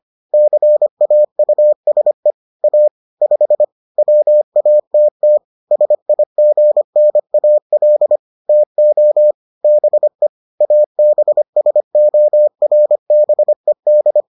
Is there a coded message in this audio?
Yes: Morse code from 0.3 to 14.2 s, heard as a keyed tone, about 25 wpm, 600 Hz. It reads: CAUSE A 5 WATT SIGNAL TO BE ABSORBED